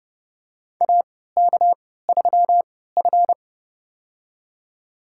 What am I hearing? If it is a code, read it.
Morse code, heard as a keyed tone, about 30 wpm, spaced as for 17 wpm, 700 Hz. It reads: AK3F
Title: USA Call Signs - 4 Characters - 3x Character Spacing - 30wpm